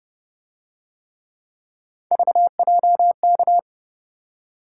Morse code, heard as a keyed tone, about 29 wpm, 700 Hz. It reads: VJK